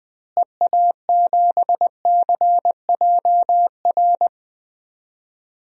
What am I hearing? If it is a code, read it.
Morse code, heard as a keyed tone, about 20 wpm, 700 Hz. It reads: EA7CJR
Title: DXCC-100 Call Signs - 6 Characters - Repeated 3x - 20wpm